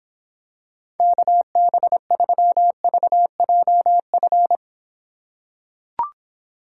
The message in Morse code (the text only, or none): KB3VJF